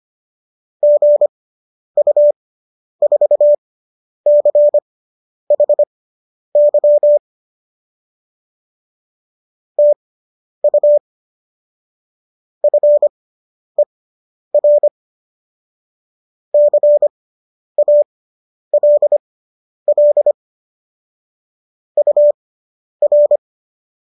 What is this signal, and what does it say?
Morse code, heard as a keyed tone, about 25 wpm, spaced as for 10 wpm, 590 Hz. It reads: GU4CHY TU FER CALL UR